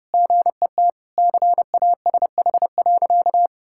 Morse code, heard as a keyed tone, about 32 wpm, 700 Hz. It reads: GET CASH.
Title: Sentences from Top 1000 Words - Review Part 5 - 30wpm